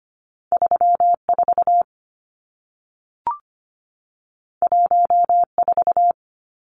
Morse code, heard as a keyed tone, about 25 wpm, 700 Hz. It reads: 34 14
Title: Numbers - 2-Digit 25wpm